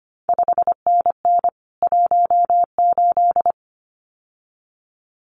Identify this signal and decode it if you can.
Morse code, heard as a keyed tone, about 25 wpm, 700 Hz. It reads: 5NN 18
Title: CQ WW DX Contest Exchange Encore - 25wpm